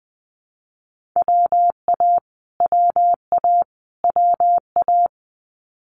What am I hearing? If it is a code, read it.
Morse code, heard as a keyed tone, about 19 wpm, 700 Hz. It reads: WA WA WA
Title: Multi-letter Phonemes Repeated 3x 20wpm